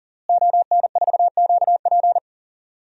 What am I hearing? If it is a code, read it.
Morse code, heard as a keyed tone, about 40 wpm, 700 Hz. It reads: ON4QP
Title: DXCC-100 Call Signs - 5 Characters - Repeated 2x - 40wpm